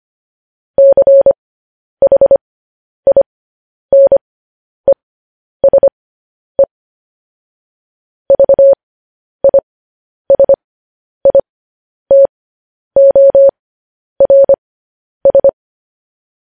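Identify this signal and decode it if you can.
Morse code, heard as a keyed tone, about 25 wpm, spaced as for 10 wpm, 550 Hz. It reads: CHINESE VISITORS
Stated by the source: Morse code news headlines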